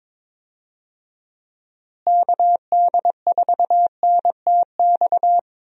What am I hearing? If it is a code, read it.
Morse code, heard as a keyed tone, about 21 wpm, 700 Hz. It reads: KD4NTX
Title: USA Call Signs - 6 Characters - Repeated 2x - 22wpm